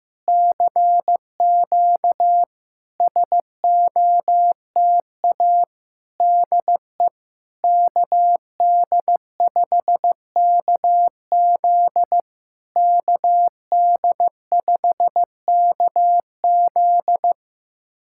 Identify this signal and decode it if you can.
Morse code, heard as a keyed tone, about 15 wpm, 700 Hz. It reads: CQ SOTA DE KD5KZ KD5KZ
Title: SOTA QSO Practice 15wpm